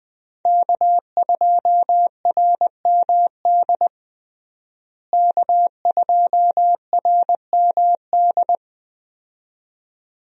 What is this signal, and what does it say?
Morse code, heard as a keyed tone, about 20 wpm, 700 Hz. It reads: K2RMD K2RMD
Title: USA Call Signs - 5 Characters - Repeated 2x - 20wpm